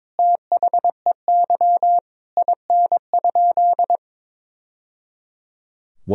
Morse code, heard as a keyed tone, about 22 wpm, 700 Hz. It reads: THEY IN?
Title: Sentences from Top 1000 Words - Review Part 2 - 22wpm